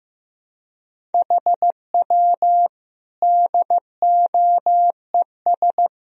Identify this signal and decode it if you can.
Morse code, heard as a keyed tone, about 15 wpm, 700 Hz. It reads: HW DOES
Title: Sentences from Top 400 Words and Abbreviations 15wpm